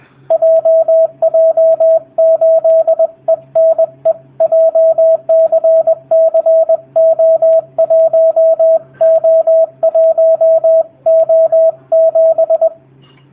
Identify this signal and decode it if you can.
Morse code, heard as a keyed tone, about 20 wpm, 650 Hz, with radio noise. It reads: JJ8ENE JCCO1O1O7
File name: jh8cbh / 2019cw / sp20- w